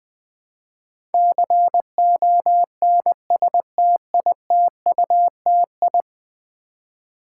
Morse code, heard as a keyed tone, about 20 wpm, 700 Hz. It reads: CONSTITUTI
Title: Build A Word - 5 Letters At A Time - 12-Letter Words - 20wpm